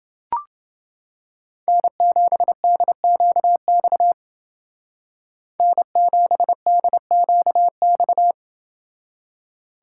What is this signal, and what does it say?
Morse code, heard as a keyed tone, about 28 wpm, 700 Hz. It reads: N7DQX N7DQX